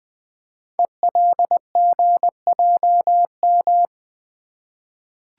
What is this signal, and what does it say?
Morse code, heard as a keyed tone, about 20 wpm, 700 Hz. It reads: ELGJM
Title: Code Groups - 5 Letters - 20wpm